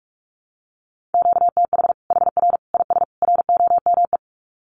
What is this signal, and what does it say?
Morse code, heard as a keyed tone, about 44 wpm, 700 Hz. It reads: QTH HR IS ROME
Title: Common QSO Phrases 45wpm